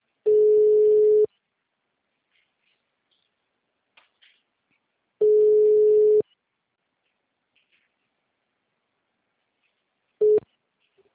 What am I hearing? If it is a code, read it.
Ringback tone (Europe, 425 Hz).